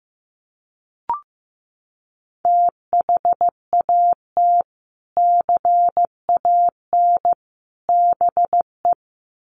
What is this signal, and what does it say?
Morse code, heard as a keyed tone, about 15 wpm, 700 Hz. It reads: THAT CAN BE
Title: Sets of 3 Words 15wpm